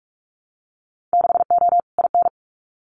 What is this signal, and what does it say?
Morse code, heard as a keyed tone, about 44 wpm, 700 Hz. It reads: 6O IN
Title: Winter Field Day Exchanges Encore 45wpm